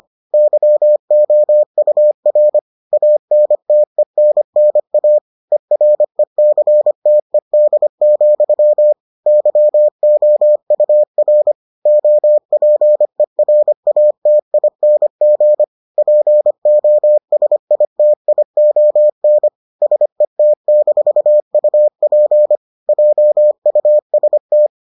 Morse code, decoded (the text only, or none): YOUR ANTENNA ERECTED, YOUR OPERATING POSITION SET-UP JUST